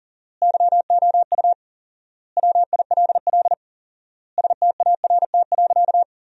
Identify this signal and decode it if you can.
Morse code, heard as a keyed tone, about 37 wpm, 700 Hz. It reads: YOU WILL START.
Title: 4x Word-Spacing - Sentences from Top 300 Words - 40wpm